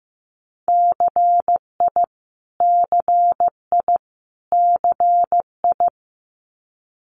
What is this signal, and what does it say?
Morse code, heard as a keyed tone, about 15 wpm, 700 Hz. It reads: CI CI CI
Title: Multi-letter Phonemes Repeated 3x 15wpm